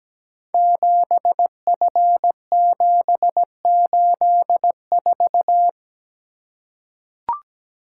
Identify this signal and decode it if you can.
Morse code, heard as a keyed tone, about 17 wpm, 690 Hz. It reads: 7F784
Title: Code Groups - 5 Letter:Numbers - 17wpm